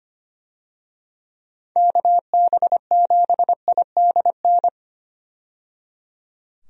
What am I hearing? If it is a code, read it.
Morse code, heard as a keyed tone, about 25 wpm, 700 Hz. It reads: KB7IDN
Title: USA Call Signs - 6 Characters - 25wpm